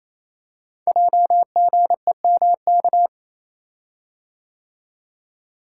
Morse code, heard as a keyed tone, about 27 wpm, 700 Hz. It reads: JGEMK
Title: Code Groups - 5 Letters - 28wpm